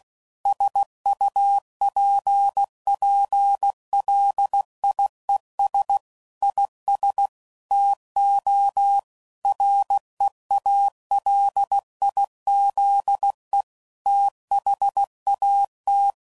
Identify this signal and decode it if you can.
Morse code, heard as a keyed tone, about 16 wpm, 800 Hz. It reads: SUPPLIES IS TO REALIZE THAT
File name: CW Code Groups @15WPM / Technical Subjects / PwrSupply